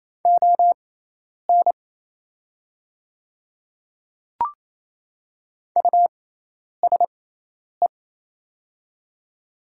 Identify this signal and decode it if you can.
Morse code, heard as a keyed tone, about 28 wpm, spaced as for 10 wpm, 700 Hz. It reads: ON USE
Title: 6x Character Spacing - Top 200 Words - 28wpm